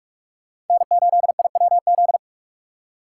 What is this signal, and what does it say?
Morse code, heard as a keyed tone, about 44 wpm, 700 Hz. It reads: N8IWZ